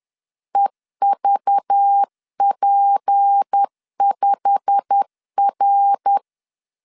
Morse code, heard as a keyed tone, about 10 wpm, 800 Hz. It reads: EVP5R
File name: LEZIONI / 8 YJ45 / 8 YJ45 10wpm 2549